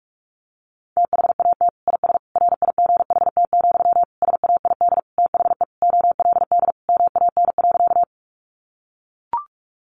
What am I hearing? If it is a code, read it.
Morse code, heard as a keyed tone, about 44 wpm, 700 Hz. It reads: THAT IS RIGHT, SAID THE OLD MAN.
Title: Sentences from Top 200 Words 45wpm